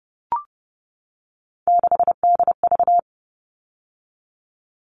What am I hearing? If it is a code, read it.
Morse code, heard as a keyed tone, about 30 wpm, 700 Hz. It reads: 6DV
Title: All Cut Numbers - 3-Digit 30wpm